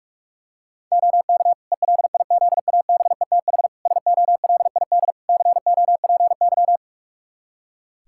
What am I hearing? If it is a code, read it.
Morse code, heard as a keyed tone, about 46 wpm, 700 Hz. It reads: OK ELIZABETH SOLID COPY
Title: Practice QSO - Exchange 3 - More Info - 45wpm